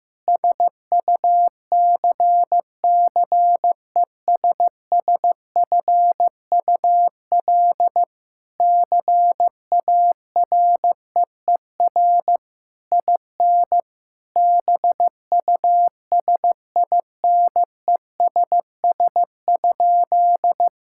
Morse code, heard as a keyed tone, about 15 wpm, 700 Hz. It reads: SUCCESSFUL CAREER IN BUSINESS?
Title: Sentences from Top 1000 Words - Review Part 6 - 15wpm